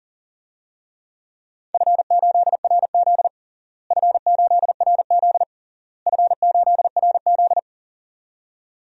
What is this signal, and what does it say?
Morse code, heard as a keyed tone, about 40 wpm, 700 Hz. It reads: F8RZ F8RZ F8RZ
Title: DXCC-100 Call Signs - 4 Characters - Repeated 3x - 40wpm